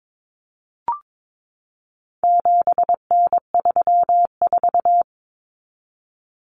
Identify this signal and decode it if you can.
Morse code, heard as a keyed tone, about 22 wpm, 700 Hz. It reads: 7N34
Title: Common Cut Numbers - 4-Digit 22wpm